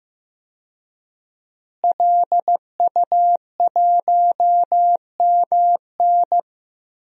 Morse code, heard as a keyed tone, about 15 wpm, 700 Hz. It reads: LU1MN
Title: DXCC-100 Call Signs - 5 Characters - Repeated 2x - 15wpm